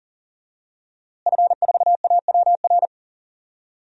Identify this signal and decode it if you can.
Morse code, heard as a keyed tone, about 40 wpm, 700 Hz. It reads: F4AWR
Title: DXCC-100 Call Signs - 5 Characters - Repeated 2x - 40wpm